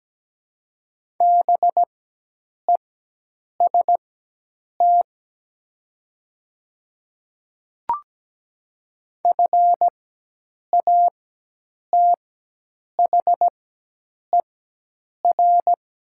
Morse code, heard as a keyed tone, about 17 wpm, spaced as for 8 wpm, 700 Hz. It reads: BEST FATHER
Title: 4x Character Spacing - Top 500 Words - 17wpm